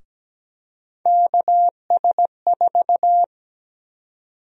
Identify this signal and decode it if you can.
Morse code, heard as a keyed tone, about 17 wpm, 700 Hz. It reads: KS4
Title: USA Call Sign Prefixes - Repeated 2x - 17wpm